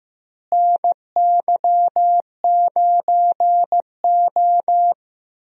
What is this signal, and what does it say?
Morse code, heard as a keyed tone, about 15 wpm, 700 Hz. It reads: NY9O
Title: USA Call Signs - 4 Characters - Repeated 3x - 15wpm